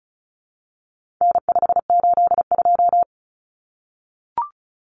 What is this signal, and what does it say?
Morse code, heard as a keyed tone, about 34 wpm, 700 Hz. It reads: N582
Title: Common Cut Numbers - 4-Digit 35wpm